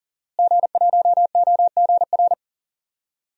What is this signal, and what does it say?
Morse code, heard as a keyed tone, about 40 wpm, 700 Hz. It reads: G1OGR
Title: DXCC-100 Call Signs - 5 Characters - Repeated 2x - 40wpm